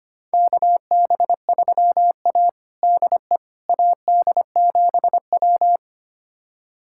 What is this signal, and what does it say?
Morse code, heard as a keyed tone, about 25 wpm, 700 Hz. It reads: KB3A DE AD7W
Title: USA Callsign DE Callsign - 4 Characters - 25wpm